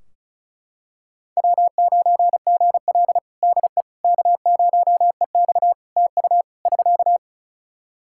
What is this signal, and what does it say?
Morse code, heard as a keyed tone, about 34 wpm, 700 Hz. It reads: W9GL DE K0EX TU <SK>
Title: Practice QSO - Exchange 4 - Ending - 35wpm